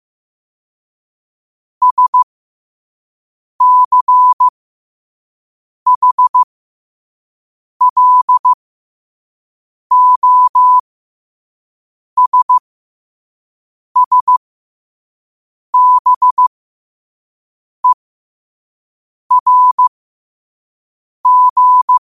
Morse code, heard as a keyed tone, about 15 wpm, spaced as for 5 wpm, 1000 Hz. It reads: SCHLOSSBERG
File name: qotdfiles / 5wpm